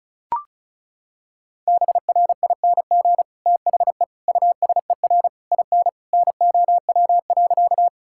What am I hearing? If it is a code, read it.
Morse code, heard as a keyed tone, about 36 wpm, 700 Hz. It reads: BRING THE USER IN NOW.
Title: Sentences from Top 1000 Words - Review Part 5 - 35wpm